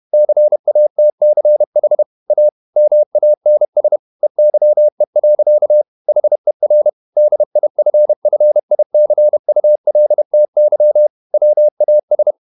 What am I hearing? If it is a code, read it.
Morse code, heard as a keyed tone, about 31 wpm, 600 Hz. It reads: CATCH A MANS EYE. HER DIFFICULTY WAS
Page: Never 31wpm | CW med Gnister